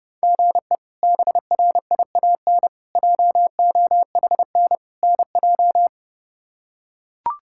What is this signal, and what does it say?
Morse code, heard as a keyed tone, about 30 wpm, 700 Hz. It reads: GE BRIAN JOHN NJ
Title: K1USN SST QSO Practice - 30wpm